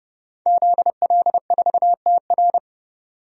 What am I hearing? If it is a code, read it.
Morse code, heard as a keyed tone, about 30 wpm, 700 Hz. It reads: ZL4TR
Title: DXCC-100 Call Signs - 5 Characters - Repeated 2x - 30wpm